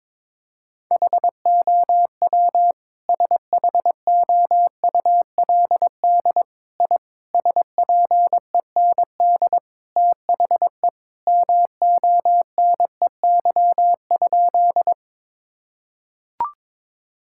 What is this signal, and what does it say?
Morse code, heard as a keyed tone, about 22 wpm, 700 Hz. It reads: HOW SHOULD I SPEND THE MONEY?
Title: Sentences from Top 1000 Words - Review Part 5 - 22wpm